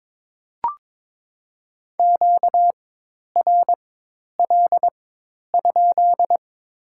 Morse code, heard as a keyed tone, about 22 wpm, spaced as for 10 wpm, 700 Hz. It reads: QRL?
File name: Top 90 QSO Elements - 4x Character Spacing - 22wpm